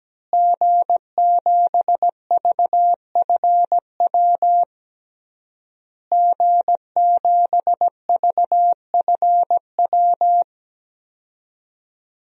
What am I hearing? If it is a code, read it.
Morse code, heard as a keyed tone, about 17 wpm, 700 Hz. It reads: G7VFW G7VFW
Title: DXCC-100 Call Signs - 5 Characters - Repeated 3x - 17wpm